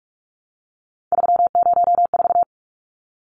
Morse code, heard as a keyed tone, about 44 wpm, 700 Hz. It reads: 304
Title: Numbers - 3-Digit 45wpm